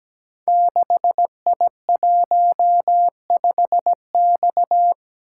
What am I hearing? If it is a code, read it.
Morse code, heard as a keyed tone, about 17 wpm, 700 Hz. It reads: 6I15X
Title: Code Groups - 5 Letter:Numbers - 17wpm